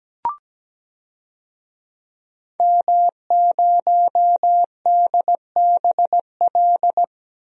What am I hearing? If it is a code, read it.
Morse code, heard as a keyed tone, about 17 wpm, 700 Hz. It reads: M0DBL
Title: DXCC-100 Call Signs - 5 Characters - 17wpm